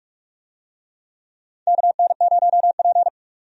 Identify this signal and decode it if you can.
Morse code, heard as a keyed tone, about 46 wpm, 700 Hz. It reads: KN0P
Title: USA Call Signs - 4 Characters - 45wpm